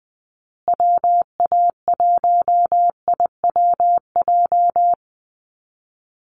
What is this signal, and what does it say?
Morse code, heard as a keyed tone, about 20 wpm, 700 Hz. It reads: WA1IWJ
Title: USA Call Signs Repeated 2x 20wpm